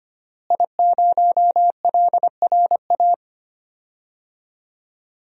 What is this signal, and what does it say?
Morse code, heard as a keyed tone, about 25 wpm, 700 Hz. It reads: I0LRA
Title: DXCC-100 Call Signs - 5 Characters - Repeated 3x - 25wpm